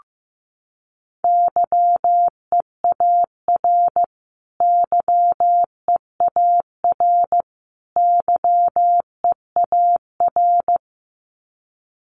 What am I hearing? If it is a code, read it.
Morse code, heard as a keyed tone, about 15 wpm, 700 Hz. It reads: YEAR YEAR YEAR
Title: Top 200 Words Repeated 15wpm